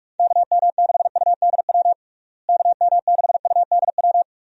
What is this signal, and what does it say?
Morse code, heard as a keyed tone, about 42 wpm, 700 Hz. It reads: KM6UDW KM6UDW
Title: USA Call Signs - 6 Characters - Repeated 3x - 45wpm